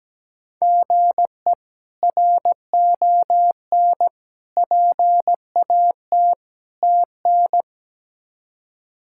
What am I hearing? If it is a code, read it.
Morse code, heard as a keyed tone, about 17 wpm, 700 Hz. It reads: GE RON PAT TN
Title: K1USN SST QSO Practice - 17wpm